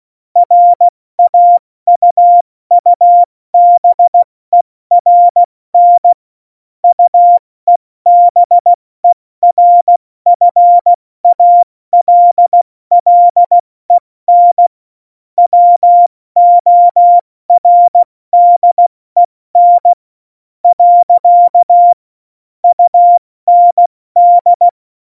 Morse code, heard as a keyed tone, about 16 wpm, 700 Hz. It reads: RAUUBERN UEBERFALLEN WORDEN . UND
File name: Morsekurs / Witze / Witz 26. Februar 2019, 70 F80